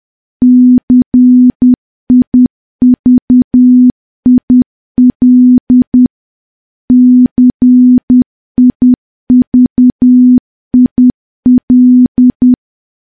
Morse code, heard as a keyed tone, about 10 wpm, 250 Hz. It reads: CIVIL CIVIL
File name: morse / morse-training-10WPM